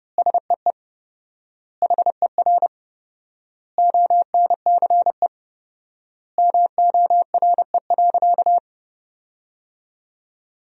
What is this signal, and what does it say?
Morse code, heard as a keyed tone, about 30 wpm, 700 Hz. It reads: SEE HER ONCE MORE.
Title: 4x Word-Spacing - Sentences from Top 500 Words - 30wpm